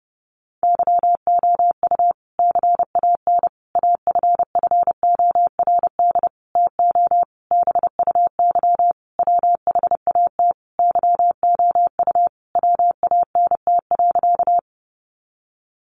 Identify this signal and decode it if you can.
Morse code, heard as a keyed tone, about 30 wpm, 700 Hz. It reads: YOU CAN AFFORD TO BUY WHAT YOU WANT.
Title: Sentences from Top 1100 Words 30wpm